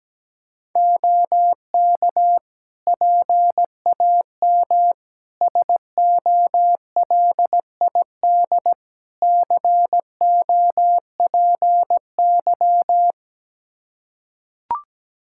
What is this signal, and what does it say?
Morse code, heard as a keyed tone, about 17 wpm, 700 Hz. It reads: OK PAM SOLID COPY
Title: Practice QSO - Exchange 3 - More Info - 17wpm